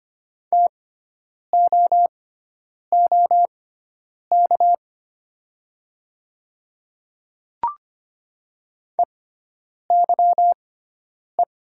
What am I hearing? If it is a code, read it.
Morse code, heard as a keyed tone, about 25 wpm, spaced as for 9 wpm, 700 Hz. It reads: TOOK EYE
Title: 6x Character Spacing - Top 300 Words - 25wpm